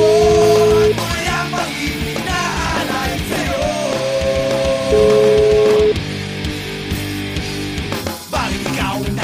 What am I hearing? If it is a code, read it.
Ringback tone (Europe, 425 Hz), with noise.